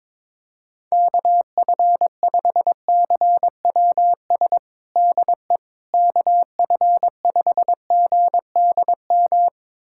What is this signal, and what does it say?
Morse code, heard as a keyed tone, about 22 wpm, 700 Hz. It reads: KF5CWS DE KF5GDM
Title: USA Callsign DE Callsign - 6 Characters - 22wpm